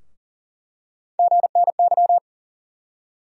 Morse code, heard as a keyed tone, about 38 wpm, 700 Hz. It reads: GNY
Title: USA Call Sign Postfixes - 3 Letters 40wpm